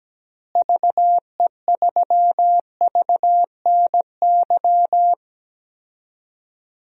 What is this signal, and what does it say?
Morse code, heard as a keyed tone, about 17 wpm, 700 Hz. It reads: VE3VNY
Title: DXCC-100 Call Signs - 6 Characters - Repeated 3x - 17wpm